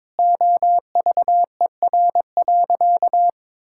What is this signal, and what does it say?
Morse code, heard as a keyed tone, about 22 wpm, 700 Hz. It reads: OVER.
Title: Sentences from Top 1000 Words - Review Part 6 - 22wpm